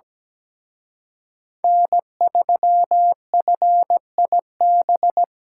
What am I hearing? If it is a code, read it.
Morse code, heard as a keyed tone, about 20 wpm, 700 Hz. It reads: N3FIB
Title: Speed Racing - USA 1x3 Call Signs 17wpm